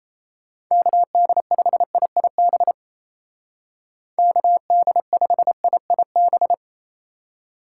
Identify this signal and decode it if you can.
Morse code, heard as a keyed tone, about 37 wpm, 700 Hz. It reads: KD5IIB KD5IIB